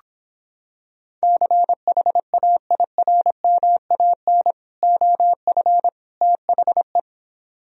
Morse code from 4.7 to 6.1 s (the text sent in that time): OF